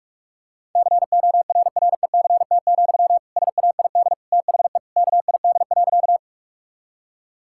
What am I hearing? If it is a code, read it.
Morse code, heard as a keyed tone, about 47 wpm, 700 Hz. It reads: CORRECT, SAID THE KID.
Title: Sentences from Top 1600 Words 45wpm